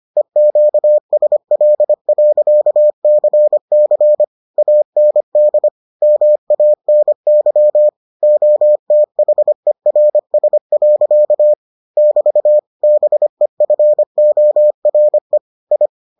Morse code, heard as a keyed tone, about 25 wpm, 600 Hz. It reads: EQSL.CC AND MANY OTHERS. <BT> BEFORE I